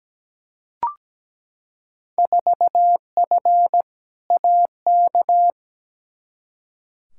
Morse code, heard as a keyed tone, about 17 wpm, 700 Hz. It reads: 4F AK
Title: Field Day Exchanges Encore - 17wpm